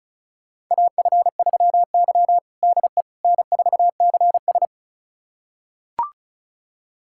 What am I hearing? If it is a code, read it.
Morse code, heard as a keyed tone, about 36 wpm, 700 Hz. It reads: AF3Y DE N4CS
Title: USA Callsign DE Callsign - 4 Characters - 35wpm